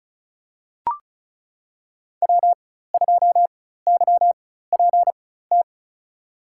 Morse code, heard as a keyed tone, about 34 wpm, spaced as for 16 wpm, 700 Hz. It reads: W2YPT
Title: USA Call Signs - 5 Characters - 4x Character Spacing - 35wpm